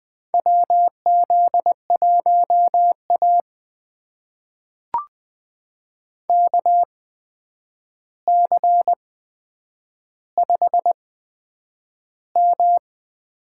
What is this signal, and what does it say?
Morse code, heard as a keyed tone, about 20 wpm, spaced as for 9 wpm, 700 Hz. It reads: WZ1A KC5M
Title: USA Call Signs - 4 Characters - 8x Character Spacing - 20wpm